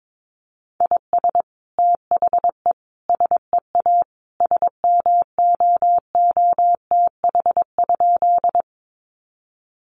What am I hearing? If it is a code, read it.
Morse code, heard as a keyed tone, about 22 wpm, 700 Hz. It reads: IS THE SEA SMOOTH?
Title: Sentences from Top 1400 Words 22wpm